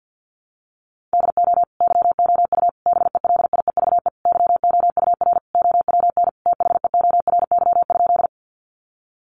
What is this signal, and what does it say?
Morse code, heard as a keyed tone, about 50 wpm, 700 Hz. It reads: DO YOU BELIEVE YOUR OWN THEORY?
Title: Sentences from Top 1000 Words 50wpm